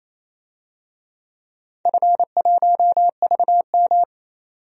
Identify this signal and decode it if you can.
Morse code, heard as a keyed tone, about 27 wpm, 700 Hz. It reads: F1VM
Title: DXCC-100 Call Signs - 4 Characters - 28wpm